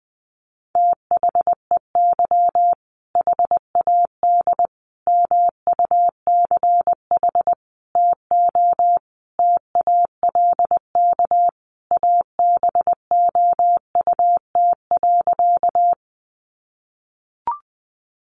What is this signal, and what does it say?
Morse code, heard as a keyed tone, about 20 wpm, 700 Hz. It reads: THEY HAD MUCH TO TALK ABOUT.
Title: Sentences from Top 400 Words 20wpm